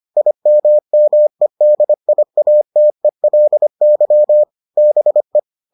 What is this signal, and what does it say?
Morse code, heard as a keyed tone, about 25 wpm, 600 Hz. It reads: IMMEDIATELY BE